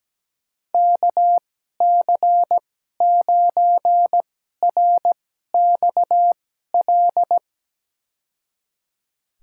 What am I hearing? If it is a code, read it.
Morse code, heard as a keyed tone, about 17 wpm, spaced as for 12 wpm, 700 Hz. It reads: KC9RXL
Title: USA Call Signs - 6 Characters - 2x Character Spacing - 17wpm